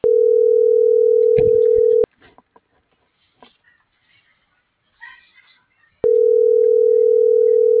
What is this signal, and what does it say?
Ringback tone (North America), 440 + 480 Hz.